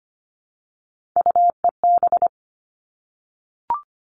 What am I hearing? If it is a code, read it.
Morse code, heard as a keyed tone, about 25 wpm, 700 Hz. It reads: UEB
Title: All Cut Numbers - 3-Digit 25wpm